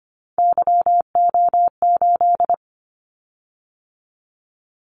Morse code, heard as a keyed tone, about 25 wpm, 700 Hz. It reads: YO8
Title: Callsign Prefixes for Top 25 DXCC - 25wpm